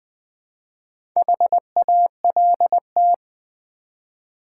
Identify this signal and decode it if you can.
Morse code, heard as a keyed tone, about 20 wpm, 700 Hz. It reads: HALT
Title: Build A Word - 1 Letter At A Time - 3 to 5-Letter Words - 20wpm